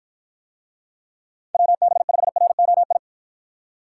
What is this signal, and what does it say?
Morse code, heard as a keyed tone, about 53 wpm, 700 Hz. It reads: WB5ROI